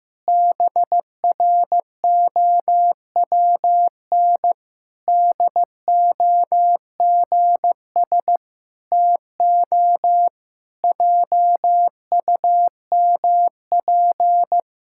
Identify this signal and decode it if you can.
Morse code, heard as a keyed tone, about 15 wpm, 700 Hz. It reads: BROWN DOGS TO JUMP
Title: Pangram Challenge v2 - 15wpm